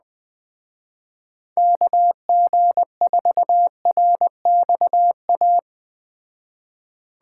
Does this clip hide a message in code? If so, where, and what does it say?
Yes: Morse code from 1.6 to 5.6 s, heard as a keyed tone, about 21 wpm, 700 Hz. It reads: KG4RXA